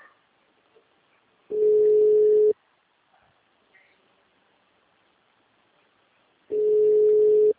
Ringback tone (Europe, 425 Hz).